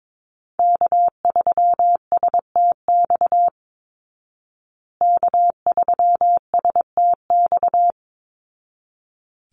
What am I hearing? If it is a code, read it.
Morse code, heard as a keyed tone, about 22 wpm, 700 Hz. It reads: K3STX K3STX
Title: Call Signs Repeated 3x - 22wpm